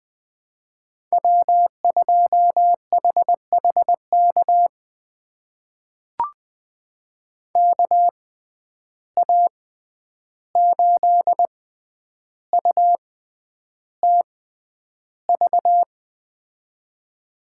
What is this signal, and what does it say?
Morse code, heard as a keyed tone, about 20 wpm, spaced as for 10 wpm, 700 Hz. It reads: W2HHK KA8UTV
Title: USA Call Signs - 6x Character Spacing - 20wpm